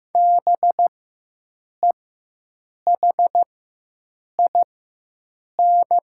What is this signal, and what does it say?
Morse code, heard as a keyed tone, about 15 wpm, spaced as for 7 wpm, 700 Hz. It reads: BEHIN
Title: 4x Character Spacing - Top 500 Words - 15wpm